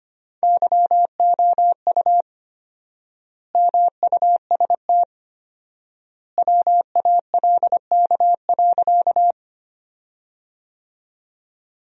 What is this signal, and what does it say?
Morse code, heard as a keyed tone, about 25 wpm, 700 Hz. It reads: YOU MUST WALK.
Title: 4x Word-Spacing - Sentences from Top 300 Words - 25wpm